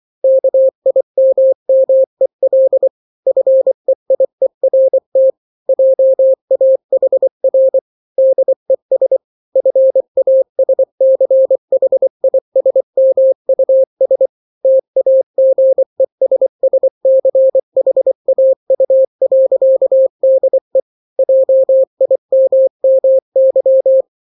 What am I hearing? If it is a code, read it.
Morse code, heard as a keyed tone, about 24 wpm, 530 Hz. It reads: KIMMEL FEIERT JAHR DES FASCHISMUS TAGESSCHAU.DE JIMMY
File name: daily / 23wpm